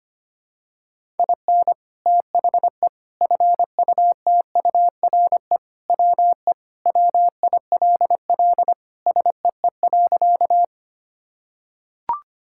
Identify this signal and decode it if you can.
Morse code, heard as a keyed tone, about 25 wpm, 700 Hz. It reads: IN THE FUTURE WE WILL SEE.
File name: practice / 25wpm / Sentences with 3-Word Phrases - 25wpm